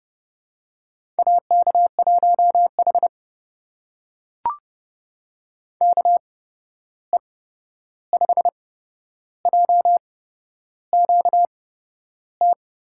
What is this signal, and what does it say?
Morse code, heard as a keyed tone, about 32 wpm, spaced as for 11 wpm, 700 Hz. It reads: AK1H KE5JQT
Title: USA Call Signs - 8x Character Spacing - 30wpm